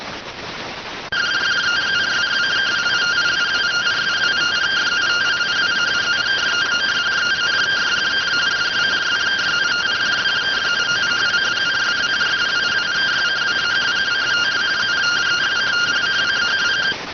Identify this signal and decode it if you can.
Two-tone teleprinter signal, about 170 Hz shift.